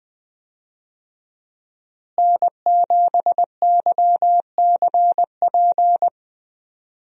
Morse code, heard as a keyed tone, about 20 wpm, 700 Hz. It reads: N7YCP
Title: USA Call Signs - 5 Characters - Repeated 2x - 20wpm